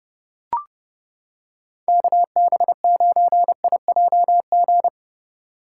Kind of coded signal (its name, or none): Morse code